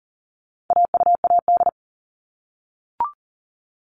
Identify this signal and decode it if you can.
Morse code, heard as a keyed tone, about 40 wpm, 700 Hz. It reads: AUAD
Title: All Cut Numbers - 4-Digit 40wpm